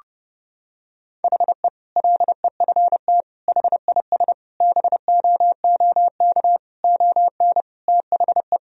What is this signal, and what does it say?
Morse code, heard as a keyed tone, about 30 wpm, 700 Hz. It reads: HE LEFT HIS BOOK ON THE
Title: Sentences from Top 1000 Words - Review Part 2 - 30wpm